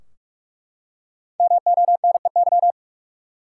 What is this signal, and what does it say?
Morse code, heard as a keyed tone, about 42 wpm, 700 Hz. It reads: MONEY
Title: Five Letter Words - 45wpm